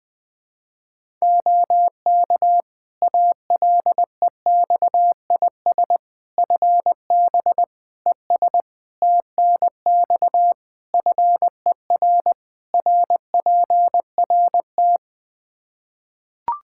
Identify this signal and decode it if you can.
Morse code, heard as a keyed tone, about 20 wpm, 700 Hz. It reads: OK ALEXIS FB ES TNX FER RPRT
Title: Practice QSO - Exchange 2 - Info - 20wpm